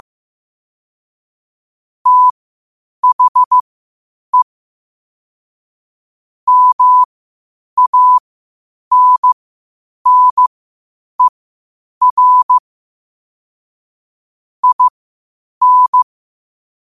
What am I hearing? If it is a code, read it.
Morse code, heard as a keyed tone, about 15 wpm, spaced as for 8 wpm, 1000 Hz. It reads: THE MANNER IN
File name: qotdfiles / 8wpm